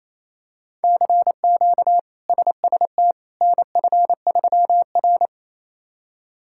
Morse code, heard as a keyed tone, about 28 wpm, 700 Hz. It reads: CQ SST NF3R
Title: K1USN SST QSO Practice - 28wpm